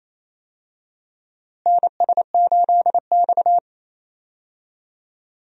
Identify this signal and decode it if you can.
Morse code, heard as a keyed tone, about 28 wpm, 700 Hz. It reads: NS8X
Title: USA Call Signs - 4 Characters - Repeated 2x - 28wpm